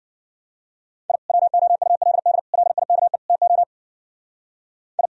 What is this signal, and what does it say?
Morse code, heard as a keyed tone, about 47 wpm, 700 Hz. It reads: I WOULD LIKE TO I
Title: Speed-Racing - Sets of 4 Words 40wpm